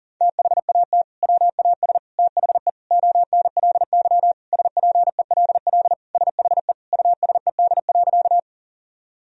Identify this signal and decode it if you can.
Morse code, heard as a keyed tone, about 40 wpm, 700 Hz. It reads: THAT WAS THE ONLY SPELL SHE USED.
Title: Sentences from Top 1000 Words - Review Part 1 - 40wpm